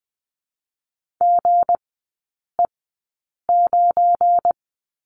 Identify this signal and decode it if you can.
Morse code, heard as a keyed tone, about 20 wpm, spaced as for 8 wpm, 700 Hz. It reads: GE9
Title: 198 Lesson 32e - Letter k - Three Characters 20wpm